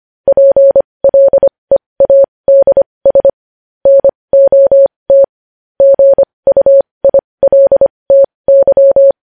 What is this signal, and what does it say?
Morse code, heard as a keyed tone, about 25 wpm, spaced as for 20 wpm, 550 Hz. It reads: PLEADS NOT GUILTY
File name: podcast / News headlines-1x-25x20